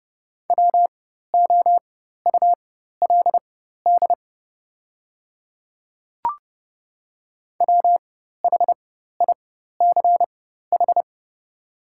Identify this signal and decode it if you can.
Morse code, heard as a keyed tone, about 30 wpm, spaced as for 14 wpm, 700 Hz. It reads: WOULD WHICH
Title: 4x Character Spacing - Top 200 Words - 30wpm